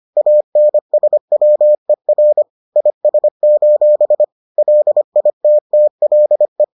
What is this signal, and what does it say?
Morse code, heard as a keyed tone, about 25 wpm, 600 Hz. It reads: ANSWER IS: LITTLE